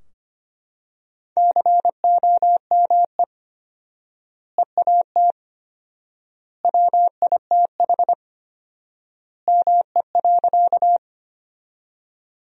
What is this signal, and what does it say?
Morse code, heard as a keyed tone, about 25 wpm, 700 Hz. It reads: COME EAT WITH ME.
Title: 4x Word-Spacing - Sentences from Top 300 Words - 25wpm